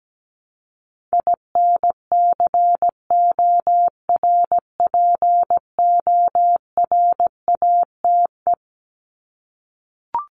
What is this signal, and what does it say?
Morse code, heard as a keyed tone, about 17 wpm, 700 Hz. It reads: INCORPORATE
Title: Top 1751-2000 Words 17wpm